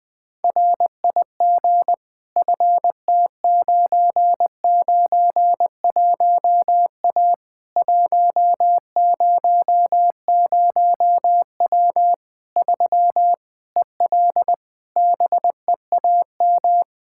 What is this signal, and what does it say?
Morse code, heard as a keyed tone, about 20 wpm, 700 Hz. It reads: RIG FT991A 100W 3 EL BEAM
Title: K1USN SST QSO Practice 20wpm